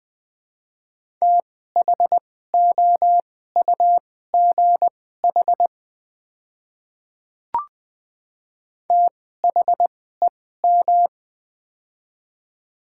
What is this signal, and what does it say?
Morse code, heard as a keyed tone, about 20 wpm, spaced as for 14 wpm, 700 Hz. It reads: THOUGH THEM
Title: 2x Character Spacing - Top 500 Words - 20wpm